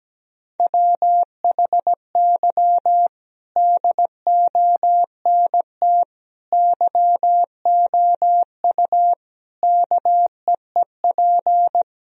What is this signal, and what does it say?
Morse code, heard as a keyed tone, about 17 wpm, 700 Hz. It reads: WHY DONT YOU KEEP
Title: Sentences from Top 1000 Words - Review Part 4 - 17wpm